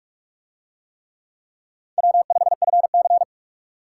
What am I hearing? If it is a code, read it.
Morse code, heard as a keyed tone, about 44 wpm, 700 Hz. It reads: W5FC